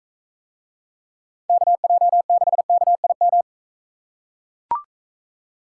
Morse code, heard as a keyed tone, about 41 wpm, 700 Hz. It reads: KJ6KIM